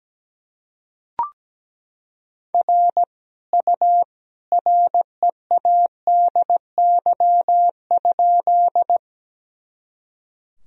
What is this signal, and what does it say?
Morse code, heard as a keyed tone, about 17 wpm, 700 Hz. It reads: R U READY?
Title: Sentences from Top 400 Words and Abbreviations 17wpm